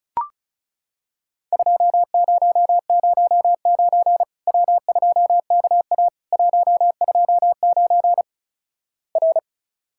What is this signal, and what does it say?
Morse code, heard as a keyed tone, about 36 wpm, 700 Hz. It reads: 2009 W2KA 129 R
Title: Worked All Europe DX Contest - QTC Exchange - 35wpm